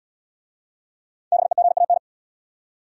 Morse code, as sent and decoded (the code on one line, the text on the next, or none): -... --. .- -.
BEGAN